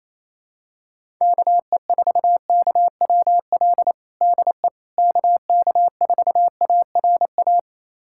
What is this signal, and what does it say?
Morse code, heard as a keyed tone, about 27 wpm, 700 Hz. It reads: KE4KWL DE KK4ARA
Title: USA Callsign DE Callsign - 6 Characters - 28wpm